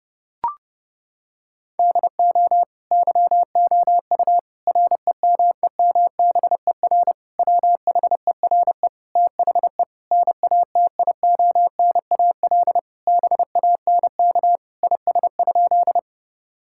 Morse code, heard as a keyed tone, about 30 wpm, 700 Hz. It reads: DO YOU REMEMBER WHERE THE NATIONAL BANK IS?
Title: Sentences from Top 1000 Words - Review Part 6 - 30wpm